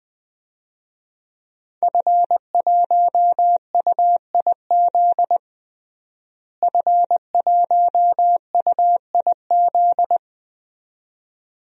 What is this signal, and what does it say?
Morse code, heard as a keyed tone, about 20 wpm, 700 Hz. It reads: F1UIZ F1UIZ
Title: DXCC-100 Call Signs - 5 Characters - Repeated 2x - 20wpm